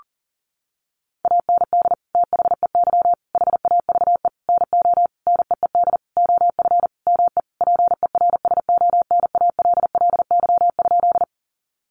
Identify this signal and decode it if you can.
Morse code, heard as a keyed tone, about 40 wpm, 700 Hz. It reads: AND THEY HAVE NO NEED OF ME PERSONALLY?
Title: Sentences from Top 1400 Words 40wpm